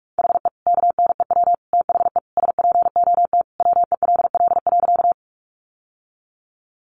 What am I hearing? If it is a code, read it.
Morse code, heard as a keyed tone, about 44 wpm, 700 Hz. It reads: HE KNEW THE SPOT WELL.
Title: Sentences from Top 1100 Words 45wpm